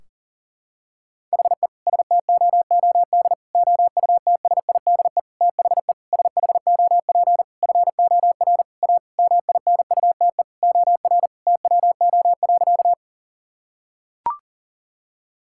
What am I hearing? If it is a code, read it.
Morse code, heard as a keyed tone, about 40 wpm, 700 Hz. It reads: HE STOOD OUTSIDE THE SHOP FOR A MINUTE OR TWO.
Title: Sentences from Top 1600 Words 40wpm